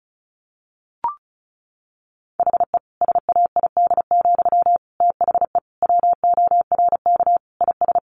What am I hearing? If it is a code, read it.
Morse code, heard as a keyed tone, about 34 wpm, 700 Hz. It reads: HE SAID, THE WORK IS
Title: Sentences from Top 400 Words 35wpm